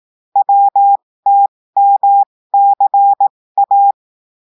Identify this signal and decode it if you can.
Morse code, heard as a keyed tone, about 18 wpm, spaced as for 15 wpm, 800 Hz. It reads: WTMCA
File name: cours-du-lundi files / cours-1580-texte-chiffre-15-mpm